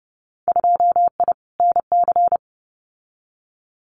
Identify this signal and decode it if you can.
Morse code, heard as a keyed tone, about 30 wpm, 700 Hz. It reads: 2I NC
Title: Winter Field Day Exchanges 30wpm